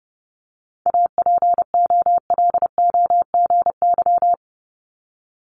Morse code, heard as a keyed tone, about 30 wpm, 690 Hz. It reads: APOLOGY
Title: Multi-letter Phonemes Repeated 3x 30wpm